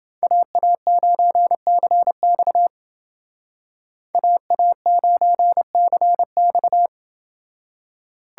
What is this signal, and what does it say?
Morse code, heard as a keyed tone, about 28 wpm, 700 Hz. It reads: AA9CX AA9CX